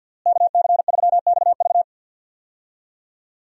Morse code, heard as a keyed tone, about 47 wpm, 700 Hz. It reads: KC3XV
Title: USA Call Signs - 5 Characters - Repeated 2x - 50wpm